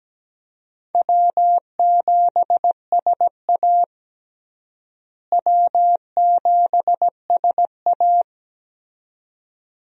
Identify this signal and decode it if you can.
Morse code, heard as a keyed tone, about 17 wpm, 700 Hz. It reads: W7SA W7SA
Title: USA Call Signs - 4 Characters - Repeated 2x - 17wpm